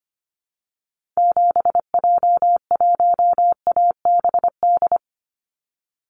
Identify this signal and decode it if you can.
Morse code, heard as a keyed tone, about 25 wpm, 700 Hz. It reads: 7J1ABD
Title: Call Signs Repeated 3x - 25wpm